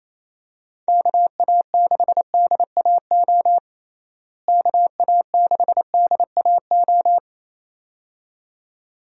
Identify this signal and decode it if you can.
Morse code, heard as a keyed tone, about 28 wpm, 700 Hz. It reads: KA6DAO KA6DAO
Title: USA Call Signs - 6 Characters - Repeated 2x - 28wpm